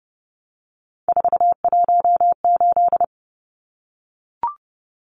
Morse code, heard as a keyed tone, about 29 wpm, 700 Hz. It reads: 418
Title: Numbers - 3-Digit 30wpm